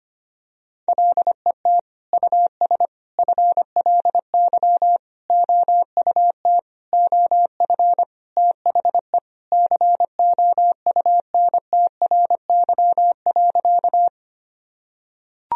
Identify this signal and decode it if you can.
Morse code, heard as a keyed tone, about 25 wpm, 700 Hz. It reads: LET US FLY OUT OF THE COUNTRY.
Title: Sentences from Top 1000 Words - Review Part 2 - 25wpm